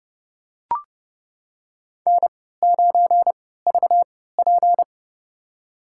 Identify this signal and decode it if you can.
Morse code, heard as a keyed tone, about 32 wpm, spaced as for 17 wpm, 700 Hz. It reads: N9VP
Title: USA Call Signs - 4 Characters - 3x Character Spacing - 30wpm